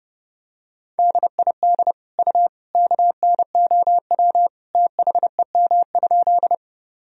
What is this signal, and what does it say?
Morse code, heard as a keyed tone, about 30 wpm, 700 Hz. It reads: DID U KNOW THEM?
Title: Sentences from Top 100 Words and Abbreviations 30wpm